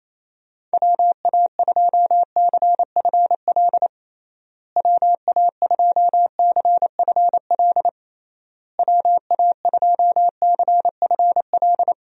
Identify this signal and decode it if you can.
Morse code, heard as a keyed tone, about 28 wpm, 700 Hz. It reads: WA2CFL WA2CFL WA2CFL
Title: USA Call Signs - 6 Characters - Repeated 3x - 28wpm